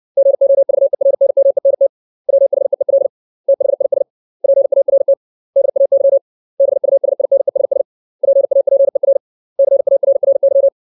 Morse code, heard as a keyed tone, about 61 wpm, 550 Hz. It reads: GOVERNMENT WHEEL THIS PART DAY BUSINESS PAPER CARRY
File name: 632 words / 632 Words 60WPM